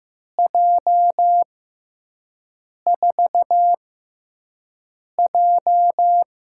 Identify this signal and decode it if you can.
Morse code, heard as a keyed tone, about 15 wpm, spaced as for 5 wpm, 700 Hz. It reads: J4J
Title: USA Call Signs - 4 Characters - 6x Character Spacing - 15wpm